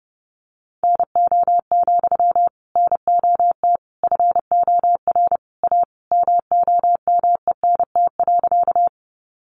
Morse code, heard as a keyed tone, about 30 wpm, 700 Hz. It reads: NO, NOT FOR A MOMENT.
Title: Sentences from Top 1000 Words 30wpm